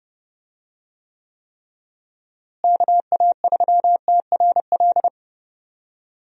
Morse code, heard as a keyed tone, about 32 wpm, 700 Hz. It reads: KA3TRL
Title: USA Call Signs - 6 Characters - Repeated 3x - 30wpm